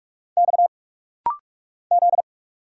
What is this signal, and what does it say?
Morse code, heard as a keyed tone, about 44 wpm, 700 Hz. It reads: X Z
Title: Single Letters - Rapid-Fire 45wpm